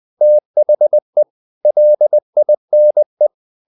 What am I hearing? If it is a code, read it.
Morse code, heard as a keyed tone, about 20 wpm, 600 Hz. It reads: THE LINE
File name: the old man and the sea part 2 20wpm